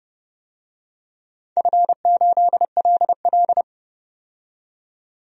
Morse code, heard as a keyed tone, about 30 wpm, 700 Hz. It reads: F8LL
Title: DXCC-100 Call Signs - 4 Characters - 30wpm